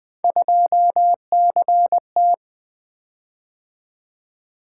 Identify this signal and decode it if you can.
Morse code, heard as a keyed tone, about 20 wpm, 700 Hz. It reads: 2CT
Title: Three Letter-Number Combinations - 20wpm